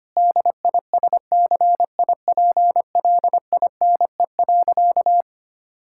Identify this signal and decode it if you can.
Morse code, heard as a keyed tone, about 25 wpm, 700 Hz. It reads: DISCIPLINE.
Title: Sentences from Top 1000 Words - Review Part 6 - 25wpm